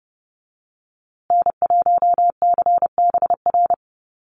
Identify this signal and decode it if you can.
Morse code, heard as a keyed tone, about 30 wpm, 700 Hz. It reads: N1CBR
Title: USA Call Signs Repeated 3x 30wpm